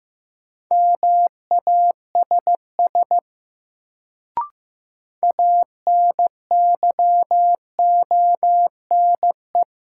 Morse code, heard as a keyed tone, about 15 wpm, 700 Hz. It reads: MASS ANYONE
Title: Sets of 5 Words Encore - New Word Drill - 15wpm